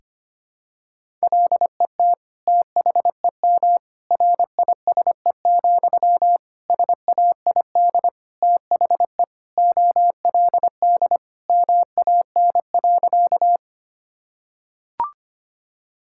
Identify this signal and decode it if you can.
Morse code, heard as a keyed tone, about 25 wpm, 700 Hz. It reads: LET THEM RISE, SAID THE OLD MAN.
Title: Sentences from Top 1000 Words - Review Part 6 - 25wpm